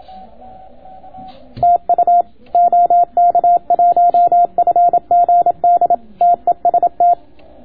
Morse code, heard as a keyed tone, about 27 wpm, 700 Hz. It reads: TU OK1FGD TEST